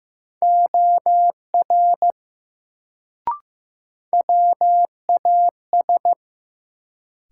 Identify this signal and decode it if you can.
Morse code, heard as a keyed tone, about 15 wpm, 700 Hz. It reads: OR WAS
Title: Sentences from Top 100 Words - New Word Drill - 15wpm